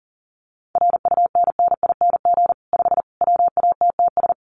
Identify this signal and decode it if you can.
Morse code, heard as a keyed tone, about 40 wpm, 700 Hz. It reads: RUNNING 5 WATTS
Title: Common QSO Phrases 40wpm